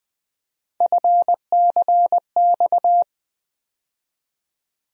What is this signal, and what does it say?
Morse code, heard as a keyed tone, about 20 wpm, 700 Hz. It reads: FCX
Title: USA Call Sign Postfixes - 3 Letters 20wpm